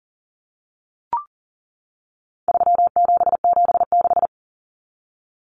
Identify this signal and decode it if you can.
Morse code, heard as a keyed tone, about 38 wpm, 700 Hz. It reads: 3776
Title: Numbers - 4-Digit 40wpm